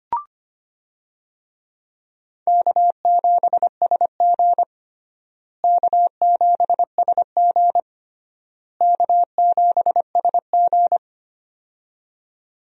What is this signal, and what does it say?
Morse code, heard as a keyed tone, about 25 wpm, 700 Hz. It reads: K7SG K7SG K7SG
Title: USA Call Signs - 4 Characters - Repeated 3x - 25wpm